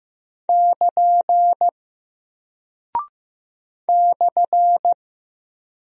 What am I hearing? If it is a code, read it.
Morse code, heard as a keyed tone, about 15 wpm, 700 Hz. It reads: <KN> /
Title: Single Character (Letter:Number:Punctuation:Prosign) Alt 15wpm